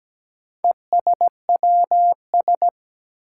Morse code, heard as a keyed tone, about 17 wpm, 700 Hz. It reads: ESWS
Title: Four Challenging Letter-Number Combinations 17wpm